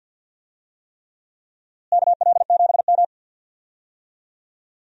Morse code, heard as a keyed tone, about 53 wpm, 700 Hz. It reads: KL7M
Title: DXCC-100 Call Signs - 4 Characters - Repeated 3x - 50wpm